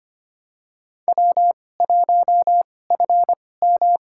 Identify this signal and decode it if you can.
Morse code, heard as a keyed tone, about 25 wpm, spaced as for 18 wpm, 700 Hz. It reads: W1FM
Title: USA Call Signs - 2x Character Spacing - 25wpm